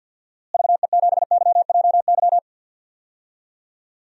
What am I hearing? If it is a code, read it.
Morse code, heard as a keyed tone, about 53 wpm, 700 Hz. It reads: VE7YJY
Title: DXCC-100 Call Signs - 6 Characters - Repeated 2x - 50wpm